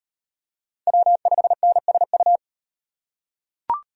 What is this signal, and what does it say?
Morse code, heard as a keyed tone, about 38 wpm, 700 Hz. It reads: W5NSU